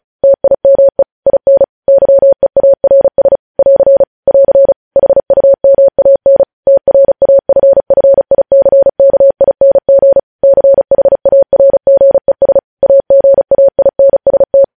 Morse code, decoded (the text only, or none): TIME IN YEARS <AR> <AR> HUMAN TRAFFICKING CHARGES AGAINST